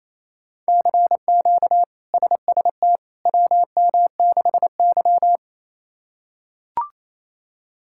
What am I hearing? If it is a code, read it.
Morse code, heard as a keyed tone, about 28 wpm, 700 Hz. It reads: CQ SST WM6Y
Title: K1USN SST QSO Practice - 28wpm